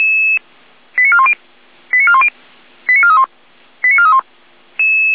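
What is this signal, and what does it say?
Five-tone selective call (ZVEI), heard as quick runs of five tones, tones of about 70 ms. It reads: 89410 89410 89431 89431
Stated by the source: ZVEI five-tone selective call received by SDR.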